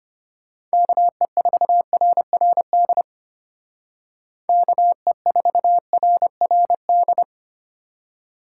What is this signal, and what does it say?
Morse code, heard as a keyed tone, about 32 wpm, 700 Hz. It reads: KE4RRD KE4RRD